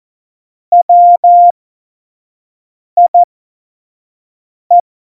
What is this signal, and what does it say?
Morse code, heard as a keyed tone, about 14 wpm, spaced as for 5 wpm, 700 Hz. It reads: WIE
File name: Morsekurs / Witze / Witz 16. April 2019, 25 F70